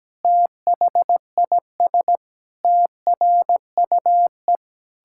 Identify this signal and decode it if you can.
Morse code, heard as a keyed tone, about 17 wpm, 700 Hz. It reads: THIS TRUE
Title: Pangram Challenge v2 - 17wpm